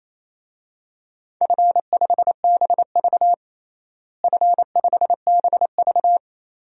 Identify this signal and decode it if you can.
Morse code, heard as a keyed tone, about 28 wpm, 700 Hz. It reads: F5BV F5BV
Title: DXCC-100 Call Signs - 4 Characters - Repeated 3x - 28wpm